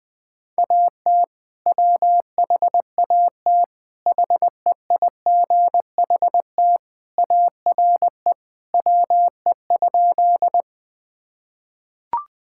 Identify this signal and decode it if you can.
Morse code, heard as a keyed tone, about 19 wpm, 700 Hz. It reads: AT WHAT HEIGHT ARE WE?
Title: Sentences from Top 1600 Words 20wpm